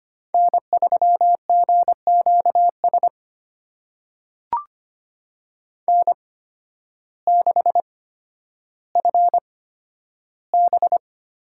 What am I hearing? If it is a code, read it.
Morse code, heard as a keyed tone, about 25 wpm, spaced as for 12 wpm, 700 Hz. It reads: N3GQS N6FB
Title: USA Call Signs - 5 Characters - 8x Character Spacing - 25wpm